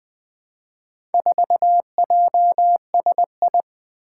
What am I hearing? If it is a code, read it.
Morse code, heard as a keyed tone, about 20 wpm, 700 Hz. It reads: 4JSI
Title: Four Challenging Letter-Number Combinations 20wpm